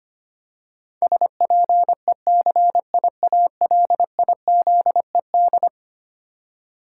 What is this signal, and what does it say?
Morse code, heard as a keyed tone, about 25 wpm, 700 Hz. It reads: SPECIALIZED
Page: morse-code-ninja-practice-25wpm